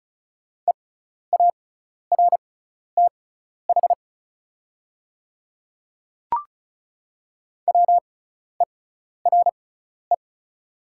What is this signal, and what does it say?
Morse code, heard as a keyed tone, about 34 wpm, spaced as for 12 wpm, 690 Hz. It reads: EARTH WERE
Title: 6x Character Spacing - Top 500 Words - 35wpm